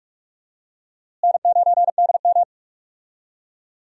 Morse code, heard as a keyed tone, about 46 wpm, 700 Hz. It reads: N9DM
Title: USA Call Signs - 4 Characters - Repeated 2x - 45wpm